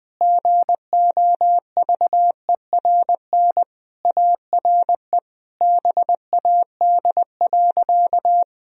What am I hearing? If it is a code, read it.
Morse code, heard as a keyed tone, about 20 wpm, 700 Hz. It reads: GOVERN ARE BAD.
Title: Sentences from Top 1000 Words - Review Part 2 - 20wpm